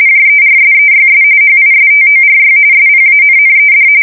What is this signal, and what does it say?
RTTY, about 45.45 baud. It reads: JERSEYS HISTORY SPANNED
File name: fsk170 / battleshipnj